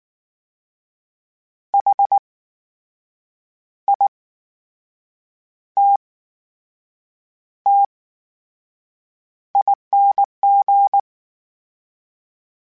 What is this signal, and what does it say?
Morse code, heard as a keyed tone, about 19 wpm, spaced as for 6 wpm, 800 Hz. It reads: HITTING
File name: Special INGs / Special INGs-mp3s / VerySlowToFaster Ing @18